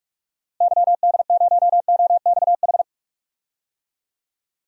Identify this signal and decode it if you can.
Morse code, heard as a keyed tone, about 46 wpm, 700 Hz. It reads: YD0OXH
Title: DXCC-100 Call Signs - 6 Characters - Repeated 2x - 45wpm